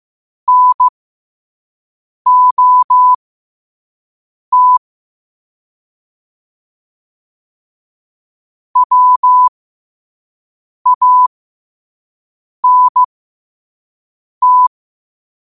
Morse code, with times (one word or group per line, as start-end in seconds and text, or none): 0.5-4.8 NOT
8.8-14.7 WANT